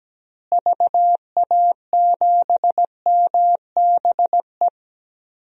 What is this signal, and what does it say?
Morse code, heard as a keyed tone, about 17 wpm, 700 Hz. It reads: VA7MBE
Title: DXCC-100 Call Signs - 6 Characters - Repeated 2x - 17wpm